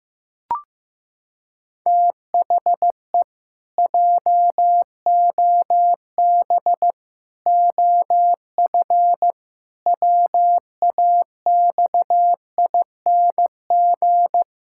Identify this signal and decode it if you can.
Morse code, heard as a keyed tone, about 15 wpm, 700 Hz. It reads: THE JOB OF WAXING
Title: Pangram Challenge v2 - 15wpm